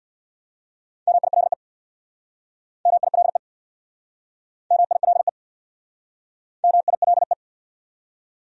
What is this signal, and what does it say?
Morse code, heard as a keyed tone, about 64 wpm, 700 Hz. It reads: MILE MILE MILE MILE